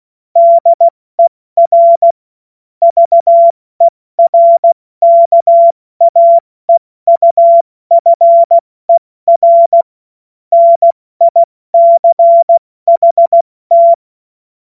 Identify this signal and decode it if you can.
Morse code, heard as a keyed tone, about 16 wpm, 680 Hz. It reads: DER VERKAEUFER NICHT
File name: Morsekurs / Witze / Witz 03. August 2021, 70 F80